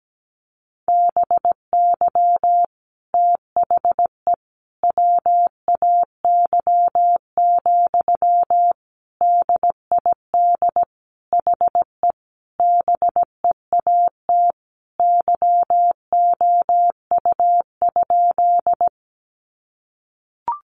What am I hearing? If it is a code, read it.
Morse code, heard as a keyed tone, about 17 wpm, 700 Hz. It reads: BY THE WAY, DID HE BEAT YOU?
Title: Sentences from Top 1300 Words 17wpm